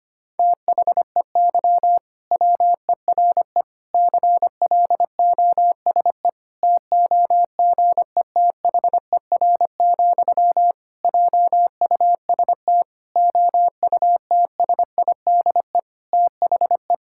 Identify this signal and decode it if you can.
Morse code, heard as a keyed tone, about 25 wpm, 700 Hz. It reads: THEY WERE CLOSE TOGETHER, JUST OUTSIDE THE
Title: Sentences from Top 1000 Words - Review Part 6 - 25wpm